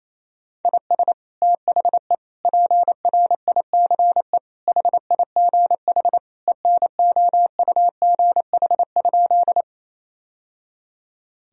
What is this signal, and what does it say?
Morse code, heard as a keyed tone, about 28 wpm, 700 Hz. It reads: IS THE PRICE HIGH ENOUGH?
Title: Sentences from Top 1000 Words - Review Part 4 - 28wpm